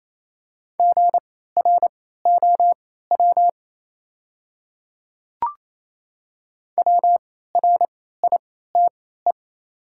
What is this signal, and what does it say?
Morse code, heard as a keyed tone, about 28 wpm, spaced as for 16 wpm, 690 Hz. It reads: GROW WRITE
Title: 3x Character Spacing - Top 500 Words - 28wpm